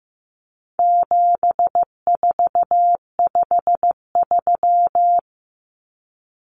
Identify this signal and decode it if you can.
Morse code, heard as a keyed tone, about 15 wpm, 700 Hz. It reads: 7453
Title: Numbers - 4-Digit 15wpm